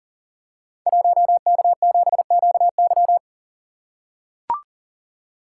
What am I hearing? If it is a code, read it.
Morse code, heard as a keyed tone, about 42 wpm, 700 Hz. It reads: 1K7QY
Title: Code Groups - 5 Letter:Numbers - 40wpm